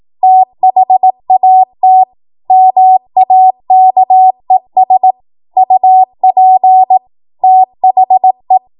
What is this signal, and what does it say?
Morse code, heard as a keyed tone, about 18 wpm, 750 Hz. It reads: THAT MAKES UP THE
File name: Morse / 18WPM